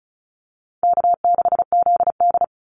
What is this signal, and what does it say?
Morse code, heard as a keyed tone, about 34 wpm, 700 Hz. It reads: K6ZD
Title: Call Signs Repeated 2x - 35wpm